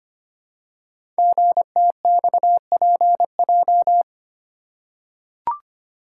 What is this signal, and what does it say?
Morse code, heard as a keyed tone, about 25 wpm, 700 Hz. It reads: GTXPJ
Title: Code Groups - 5 Letters - 25wpm